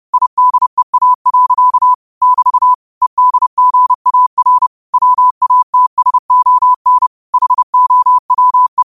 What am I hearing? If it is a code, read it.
Morse code, heard as a keyed tone, about 30 wpm, 1000 Hz. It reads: IDEA. <BT> EDGAR WATSON HOWE
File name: 30wpm